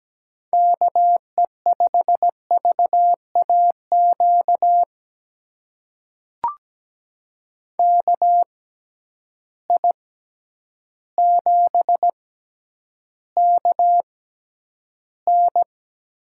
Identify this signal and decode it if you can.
Morse code, heard as a keyed tone, about 17 wpm, spaced as for 9 wpm, 700 Hz. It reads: KE5VAQ KI7KN
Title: USA Call Signs - 6 Characters - 6x Character Spacing - 17wpm